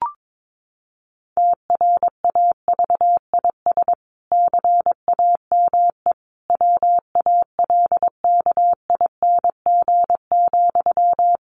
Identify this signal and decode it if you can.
Morse code, heard as a keyed tone, about 22 wpm, 700 Hz. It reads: TRAVIS CAME WALKING,
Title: A Sound Of Thunder 22wpm